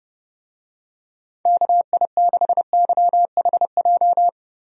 Morse code, heard as a keyed tone, about 30 wpm, 700 Hz. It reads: KI6YHJ
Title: USA Call Signs - 6 Characters - Repeated 2x - 30wpm